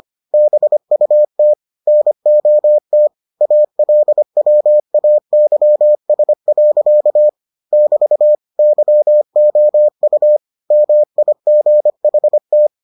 Morse code, heard as a keyed tone, about 25 wpm, 600 Hz. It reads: BUT NOT ALWAYS. <BT> YOU MIGHT